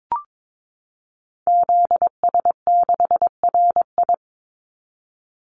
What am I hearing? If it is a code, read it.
Morse code, heard as a keyed tone, about 22 wpm, 700 Hz. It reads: ZS6RI
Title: Call Signs Repeated 3x - 22wpm